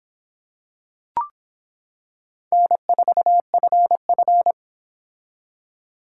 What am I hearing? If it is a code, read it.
Morse code, heard as a keyed tone, about 25 wpm, 700 Hz. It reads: N4FF